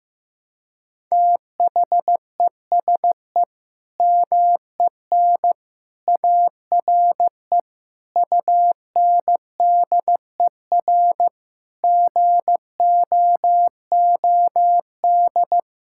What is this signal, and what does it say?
Morse code, heard as a keyed tone, about 15 wpm, 700 Hz. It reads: THESE MEN ARE UNDER GOOD
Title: Sentences from Top 1000 Words - Review Part 6 - 15wpm